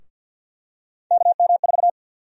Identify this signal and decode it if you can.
Morse code, heard as a keyed tone, about 53 wpm, 700 Hz. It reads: KM4
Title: USA Call Sign Prefixes - Repeated 2x - 50wpm